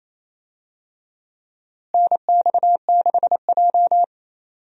Morse code, heard as a keyed tone, about 27 wpm, 700 Hz. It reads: NX6J
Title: USA Call Signs - 4 Characters - Repeated 3x - 28wpm